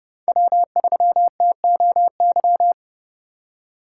Morse code, heard as a keyed tone, about 30 wpm, 700 Hz. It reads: W3TOY
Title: USA Call Signs - 2x Character Spacing - 30wpm